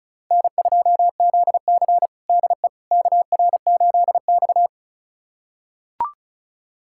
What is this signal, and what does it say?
Morse code, heard as a keyed tone, about 36 wpm, 700 Hz. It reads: N2ZC DE KR8X
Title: USA Callsign DE Callsign - 4 Characters - 35wpm